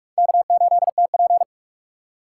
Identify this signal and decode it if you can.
Morse code, heard as a keyed tone, about 47 wpm, 700 Hz. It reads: K8TP
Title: USA Call Signs - 6x Character Spacing - 45wpm